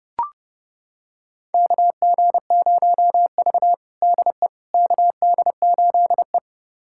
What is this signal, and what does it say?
Morse code, heard as a keyed tone, about 30 wpm, 700 Hz. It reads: KG0V DE KD8E
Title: USA Callsign DE Callsign - 4 Characters - 30wpm